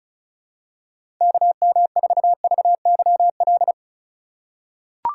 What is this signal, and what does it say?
Morse code, heard as a keyed tone, about 34 wpm, 700 Hz. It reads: KM4VYL
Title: USA Call Signs - 4x Character Spacing - 35wpm